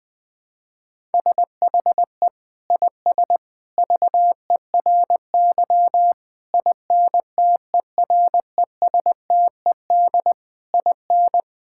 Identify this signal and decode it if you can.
Morse code, heard as a keyed tone, about 21 wpm, 700 Hz. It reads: SHE IS VERY INTERESTED IN
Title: Sentences from Top 1000 Words - Review Part 6 - 20wpm